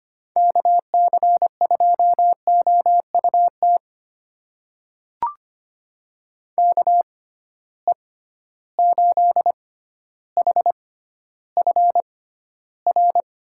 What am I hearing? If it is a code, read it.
Morse code, heard as a keyed tone, about 25 wpm, spaced as for 13 wpm, 700 Hz. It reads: KC2OUT KE8HFR
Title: USA Call Signs - 6 Characters - 6x Character Spacing - 25wpm